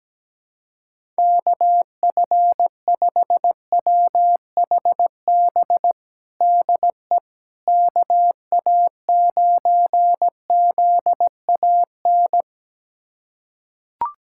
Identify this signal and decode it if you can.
Morse code, heard as a keyed tone, about 17 wpm, 700 Hz. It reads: KF5WHB DE KA9ZAN
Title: USA Callsign DE Callsign - 6 Characters - 17wpm